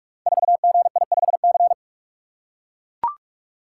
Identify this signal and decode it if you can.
Morse code, heard as a keyed tone, about 43 wpm, 700 Hz. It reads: 4GI5C
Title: Code Groups - 5 Letter:Numbers - 45wpm